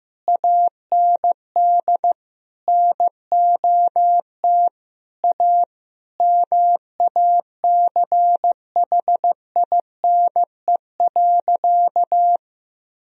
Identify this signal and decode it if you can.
Morse code, heard as a keyed tone, about 15 wpm, 700 Hz. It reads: AND NOT A MACHINE.
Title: Sentences from Top 1000 Words - Review Part 2 - 15wpm